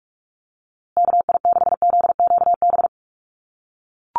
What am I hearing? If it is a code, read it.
Morse code, heard as a keyed tone, about 44 wpm, 700 Hz. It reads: KI6ZQB
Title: USA Call Signs Repeated 2x 45wpm